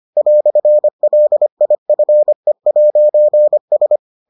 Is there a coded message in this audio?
Yes: Morse code from 0.2 to 4.0 s, heard as a keyed tone, about 25 wpm, 600 Hz. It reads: "LIFE'S